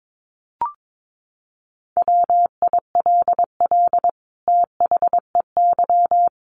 Morse code, heard as a keyed tone, about 22 wpm, 700 Hz. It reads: WILL THEY
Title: Sentences from Top 1300 Words 22wpm